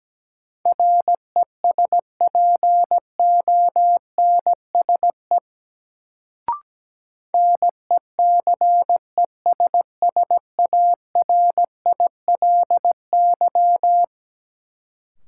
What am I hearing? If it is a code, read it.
Morse code, heard as a keyed tone, about 17 wpm, 700 Hz. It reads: RESPONSE NECESSARILY
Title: Sentences from Top 1400 Words - New Word Drill Alt - 17wpm